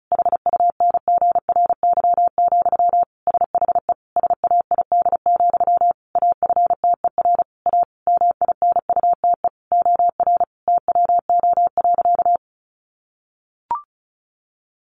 Morse code, heard as a keyed tone, about 34 wpm, 700 Hz. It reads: HUNGRY, SHE SAID, AFTER A MINUTE OR TWO.
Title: Sentences from Top 1400 Words 35wpm